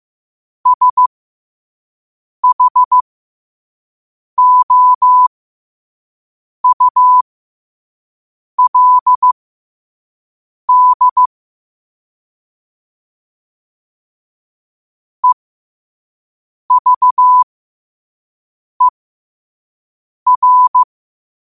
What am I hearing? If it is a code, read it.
Morse code, heard as a keyed tone, about 15 wpm, spaced as for 5 wpm, 1000 Hz. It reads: SHOULD EVER
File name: qotdfiles / 5wpm